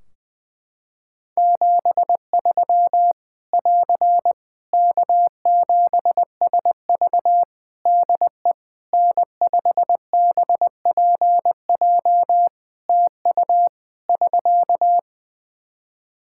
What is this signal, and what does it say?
Morse code, heard as a keyed tone, about 20 wpm, 700 Hz. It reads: 73 <AR> K7SV DE N5BPJ TU <SK>
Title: Practice QSO - Exchange 4 - Ending - 20wpm